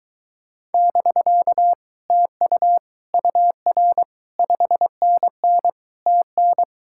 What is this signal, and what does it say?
Morse code, heard as a keyed tone, about 23 wpm, 700 Hz. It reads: <BK> TU UR 5NN TN